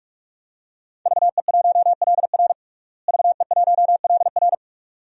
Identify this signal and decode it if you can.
Morse code, heard as a keyed tone, about 46 wpm, 700 Hz. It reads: VE1LR VE1LR
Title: DXCC-100 Call Signs - 5 Characters - Repeated 2x - 45wpm